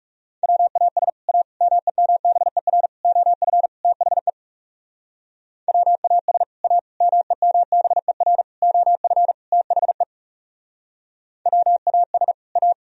Morse code, heard as a keyed tone, about 37 wpm, 700 Hz. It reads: WAS A MEMBER OF THE WAS A MEMBER OF THE WAS A
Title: Speed-Racing - Sets of 5 Words 30wpm